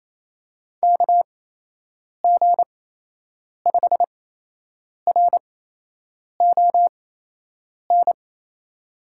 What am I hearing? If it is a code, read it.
Morse code, heard as a keyed tone, about 27 wpm, spaced as for 8 wpm, 700 Hz. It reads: KG5RON
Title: USA Call Signs - 6 Characters - 8x Character Spacing - 28wpm